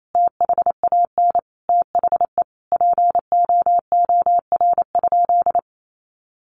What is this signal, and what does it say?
Morse code, heard as a keyed tone, about 28 wpm, 700 Hz. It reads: THAN THE POOR?
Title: Sentences from Top 1100 Words 28wpm